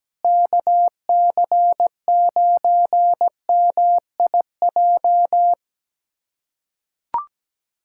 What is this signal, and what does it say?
Morse code, heard as a keyed tone, about 17 wpm, 700 Hz. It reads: KC9MIJ
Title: USA Callsign DE Callsign - 6 Characters - 17wpm